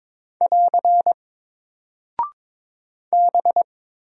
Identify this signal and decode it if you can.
Morse code, heard as a keyed tone, about 22 wpm, 700 Hz. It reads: <AR> B
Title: Single Character (Letter:Number:Punctuation:Prosign) 22wpm